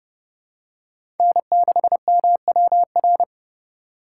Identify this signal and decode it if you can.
Morse code, heard as a keyed tone, about 30 wpm, 700 Hz. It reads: N6MWR
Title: USA Call Signs - 5 Characters - 2x Character Spacing - 30wpm